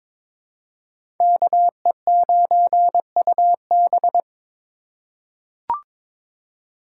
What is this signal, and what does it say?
Morse code, heard as a keyed tone, about 22 wpm, 700 Hz. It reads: KE9UB
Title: USA Call Signs - 5 Characters - 2x Character Spacing - 22wpm